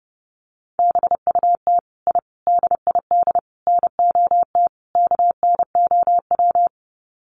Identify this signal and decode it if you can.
Morse code, heard as a keyed tone, about 28 wpm, 700 Hz. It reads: BUT I DID NOT KNOW
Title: Sets of 5 Words Encore 30wpm